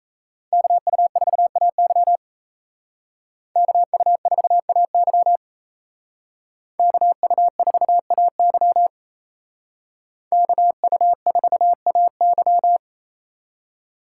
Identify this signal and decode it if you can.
Morse code, heard as a keyed tone, about 34 wpm, 700 Hz. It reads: KU4AY KU4AY KU4AY KU4AY
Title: Speed Racing - USA 2x2 Call Signs 28wpm